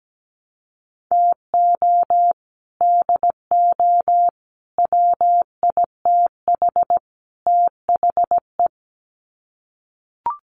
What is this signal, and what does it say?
Morse code, heard as a keyed tone, about 17 wpm, 700 Hz. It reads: TO DO WITH THE
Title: Sets of 4 Words 17wpm